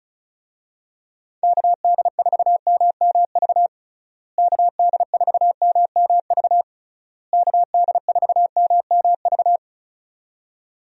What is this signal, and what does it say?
Morse code, heard as a keyed tone, about 34 wpm, 700 Hz. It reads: KD4MMV KD4MMV KD4MMV
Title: USA Call Signs - 6 Characters - Repeated 3x - 35wpm